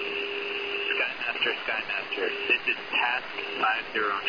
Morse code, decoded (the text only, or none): none